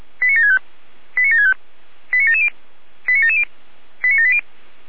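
Five-tone selective call (ZVEI), heard as quick runs of five tones, tones of about 80 ms. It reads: 89765 89765 89770 89770 89780
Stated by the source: ZVEI five-tone selective call received on an SDR.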